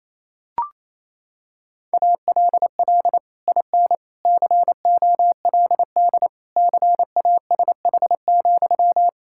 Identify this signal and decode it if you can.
Morse code, heard as a keyed tone, about 28 wpm, 700 Hz. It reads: ALL IN COLD CASH,
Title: Sentences from Top 1000 Words - Review Part 5 - 28wpm